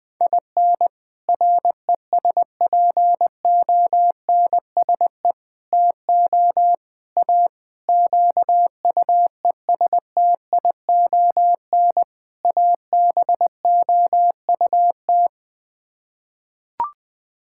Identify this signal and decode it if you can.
Morse code, heard as a keyed tone, about 21 wpm, 700 Hz. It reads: IN RESPONSE TO A QUESTION ABOUT
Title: Sets of 6 Words - 20wpm